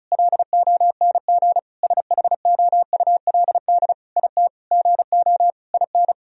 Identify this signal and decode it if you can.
Morse code, heard as a keyed tone, about 36 wpm, 700 Hz. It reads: LONG SHOULD IT GO IN
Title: Sentences from Top 1000 Words - Review Part 4 - 35wpm